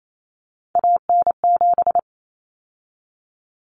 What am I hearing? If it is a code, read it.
Morse code, heard as a keyed tone, about 28 wpm, 700 Hz. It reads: AN7
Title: Common Cut Numbers - 3-Digit 28wpm